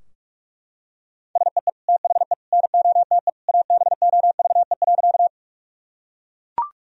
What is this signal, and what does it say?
Morse code, heard as a keyed tone, about 43 wpm, 700 Hz. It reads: SEE THE NOTE ABOVE.
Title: Sentences from Top 1000 Words - Review Part 2 - 45wpm